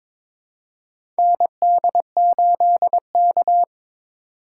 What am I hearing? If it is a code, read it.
Morse code, heard as a keyed tone, about 22 wpm, 700 Hz. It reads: ND8K
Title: USA Call Signs - 4 Characters - 4x Character Spacing - 22wpm